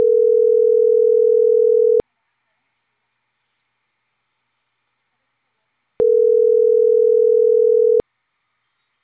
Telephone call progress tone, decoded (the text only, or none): ringback tone